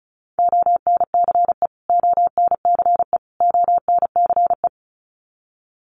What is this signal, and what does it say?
Morse code, heard as a keyed tone, about 34 wpm, 700 Hz. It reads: ONCE ONCE ONCE
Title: Top 300 Words Repeated 35wpm